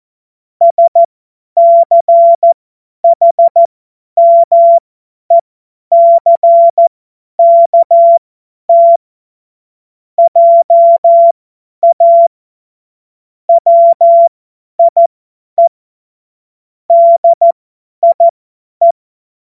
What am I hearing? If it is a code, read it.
Morse code, heard as a keyed tone, about 14 wpm, spaced as for 10 wpm, 680 Hz. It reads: SCHMECKT JA WIE DIE
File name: Morsekurs / Witze / Witz 25. Januar 2022, 50 F70